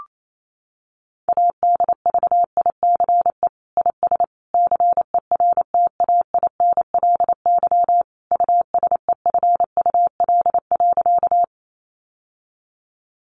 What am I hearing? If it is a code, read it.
Morse code, heard as a keyed tone, about 28 wpm, 700 Hz. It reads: ADVICE IS CERTAINLY USEFUL.
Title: Sentences from Top 1100 Words 28wpm